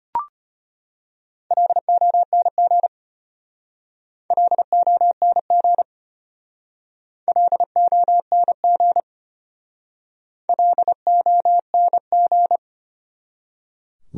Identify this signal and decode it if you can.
Morse code, heard as a keyed tone, about 30 wpm, 700 Hz. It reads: LONG LONG LONG LONG
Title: Speed-Racing Top 200 Words 25wpm